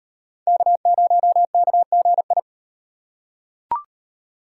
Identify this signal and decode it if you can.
Morse code, heard as a keyed tone, about 34 wpm, 700 Hz. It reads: K0KGI